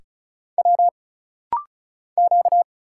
Morse code, heard as a keyed tone, about 36 wpm, 700 Hz. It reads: W Q
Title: Single Letters - Rapid-Fire 35wpm